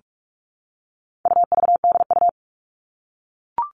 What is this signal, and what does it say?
Morse code, heard as a keyed tone, about 44 wpm, 700 Hz. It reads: UVDU
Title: All Cut Numbers - 4-Digit 45wpm